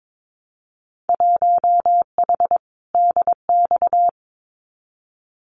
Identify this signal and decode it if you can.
Morse code, heard as a keyed tone, about 22 wpm, 700 Hz. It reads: 1H DX
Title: Winter Field Day Exchanges 22wpm